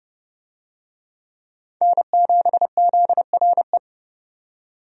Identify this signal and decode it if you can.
Morse code, heard as a keyed tone, about 31 wpm, 700 Hz. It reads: N7ZRE
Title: USA Call Signs - 5 Characters - 30wpm